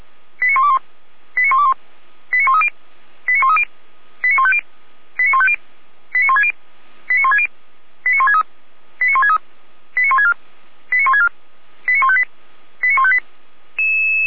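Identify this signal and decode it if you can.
Five-tone selective call (ZVEI), heard as quick runs of five tones, tones of about 80 ms. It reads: 89121 89121 89130 89130 89150 89150 89160 89160 89163 89163 89164 89164 89158 89158
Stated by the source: ZVEI five-tone selective call 89140 received on SDR, from the Feuerwehr Orlamünde fire brigade audio server.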